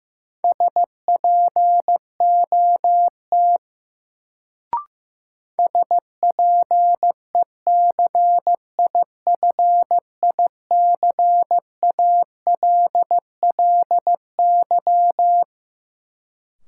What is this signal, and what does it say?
Morse code, heard as a keyed tone, about 15 wpm, 700 Hz. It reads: SPOT SPECIFICALLY
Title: Sentences from Top 1100 Words - New Word Drill - 15wpm